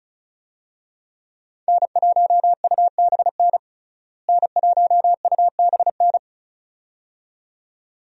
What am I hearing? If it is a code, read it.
Morse code, heard as a keyed tone, about 35 wpm, 700 Hz. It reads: N1UBN N1UBN
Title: USA Call Signs - 5 Characters - Repeated 2x - 35wpm